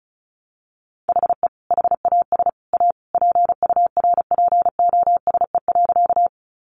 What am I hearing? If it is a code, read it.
Morse code, heard as a keyed tone, about 34 wpm, 700 Hz. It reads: HE HAS A PURPOSE.
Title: Sentences from Top 800 Words 35wpm